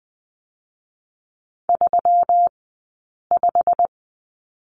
Morse code, heard as a keyed tone, about 20 wpm, 700 Hz. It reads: 3 5
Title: 120 Lesson 20d - Number 3 - Two Characters 20wpm